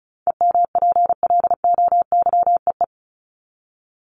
Morse code, heard as a keyed tone, about 34 wpm, 700 Hz. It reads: EMPLOYEE
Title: Top 1501-1750 Words 35wpm